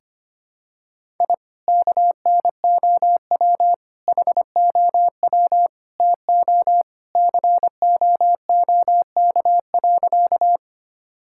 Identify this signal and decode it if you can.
Morse code, heard as a keyed tone, about 25 wpm, 700 Hz. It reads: I KNOW HOW TO COOK.
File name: practice / 25wpm / Sentences with 3-Word Phrases - 25wpm